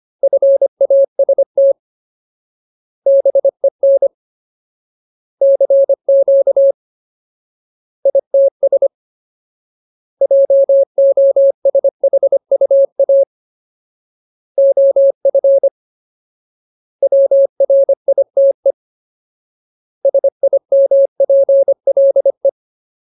Morse code, heard as a keyed tone, about 25 wpm, 550 Hz. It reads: FAST BEN CQ ITS JOSHUA OF WRITE SIMPLE
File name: Ultimate Word List / 25WPM Ultimate